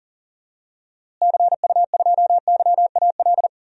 Morse code, heard as a keyed tone, about 40 wpm, 700 Hz. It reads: CU2YAL
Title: DXCC-100 Call Signs - 6 Characters - Repeated 2x - 40wpm